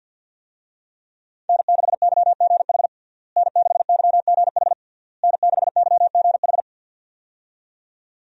Morse code, heard as a keyed tone, about 48 wpm, 700 Hz. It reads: N6YGH N6YGH N6YGH
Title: USA Call Signs - 5 Characters - Repeated 3x - 50wpm